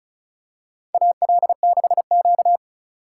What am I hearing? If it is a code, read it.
Morse code, heard as a keyed tone, about 36 wpm, 700 Hz. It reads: AL6Q
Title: USA Call Signs - 4 Characters - 2x Character Spacing - 35wpm